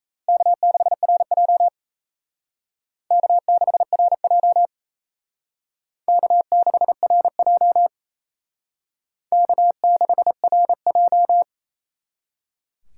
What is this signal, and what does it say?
Morse code, heard as a keyed tone, about 36 wpm, 700 Hz. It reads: K6RJ K6RJ K6RJ K6RJ